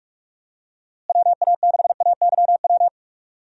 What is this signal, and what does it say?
Morse code, heard as a keyed tone, about 46 wpm, 700 Hz. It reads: WA6AYW
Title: USA Call Signs - 6 Characters - Repeated 2x - 45wpm